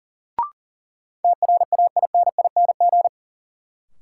Morse code, heard as a keyed tone, about 40 wpm, 700 Hz. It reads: TRAINING
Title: Sentences from Top 1000 Words - Review Part 4 - New Word Drill - 40wpm